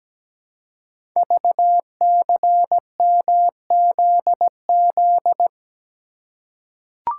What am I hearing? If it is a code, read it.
Morse code, heard as a keyed tone, about 17 wpm, 700 Hz. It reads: VCMZZ
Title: Code Groups - 5 Letters - 17wpm